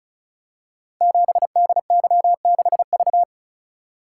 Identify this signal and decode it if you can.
Morse code, heard as a keyed tone, about 36 wpm, 690 Hz. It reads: 7DY6V
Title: Code Groups - 5 Letter:Numbers - 35wpm